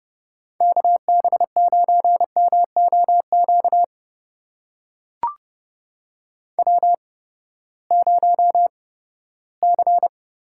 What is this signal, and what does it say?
Morse code, heard as a keyed tone, about 30 wpm, spaced as for 17 wpm, 700 Hz. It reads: KB9MOQ W0C
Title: USA Call Signs - 8x Character Spacing - 30wpm